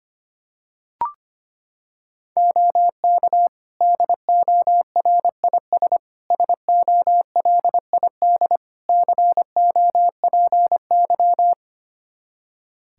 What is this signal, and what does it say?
Morse code, heard as a keyed tone, about 25 wpm, 700 Hz. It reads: OK DORIS SOLID COPY
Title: Practice QSO - Exchange 3 - More Info - 25wpm